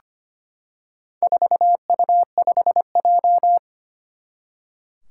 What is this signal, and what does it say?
Morse code, heard as a keyed tone, about 25 wpm, 700 Hz. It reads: 4U5J
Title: Four Challenging Letter-Number Combinations 25wpm